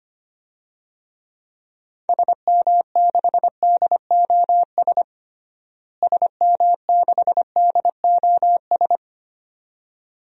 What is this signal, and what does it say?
Morse code, heard as a keyed tone, about 25 wpm, 700 Hz. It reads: SM6DOS SM6DOS
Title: DXCC-100 Call Signs - 6 Characters - Repeated 2x - 25wpm